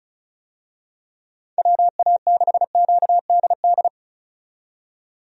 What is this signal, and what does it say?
Morse code, heard as a keyed tone, about 35 wpm, 700 Hz. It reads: WA6QDD
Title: USA Call Signs - 6 Characters - 35wpm